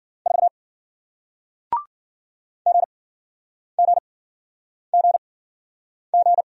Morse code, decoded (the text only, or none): 4 G G G G